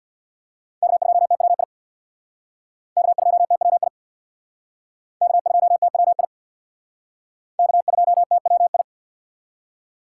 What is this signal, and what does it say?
Morse code, heard as a keyed tone, about 58 wpm, 700 Hz. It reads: K2TWI K2TWI K2TWI K2TWI
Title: Speed Racing - USA 1x3 Call Signs 50wpm